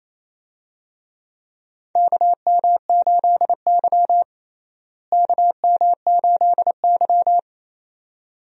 Morse code, heard as a keyed tone, about 28 wpm, 700 Hz. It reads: KM8Y KM8Y
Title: USA Call Signs - 4 Characters - Repeated 2x - 28wpm